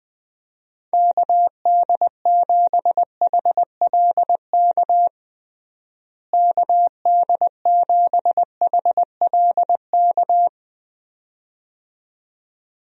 Morse code, heard as a keyed tone, about 21 wpm, 700 Hz. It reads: KD7HLK KD7HLK
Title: USA Call Signs - 6 Characters - Repeated 2x - 20wpm